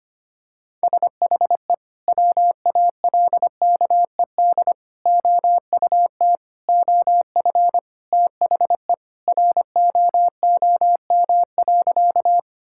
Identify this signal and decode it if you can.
Morse code, heard as a keyed tone, about 25 wpm, 700 Hz. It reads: SHE WALKED OUT OF THE ROOM.
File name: practice / 25wpm / Sentences with 3-Word Phrases - 25wpm